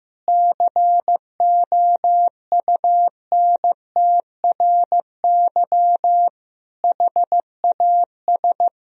Morse code, heard as a keyed tone, about 14 wpm, 700 Hz. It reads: COUNTRY HAS
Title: Sentences from Top 1000 Words - Review Part 4 - 15wpm